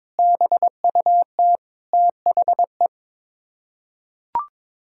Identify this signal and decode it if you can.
Morse code, heard as a keyed tone, about 22 wpm, 700 Hz. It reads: BUT THE
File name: Sets of 9 Words - 22wpm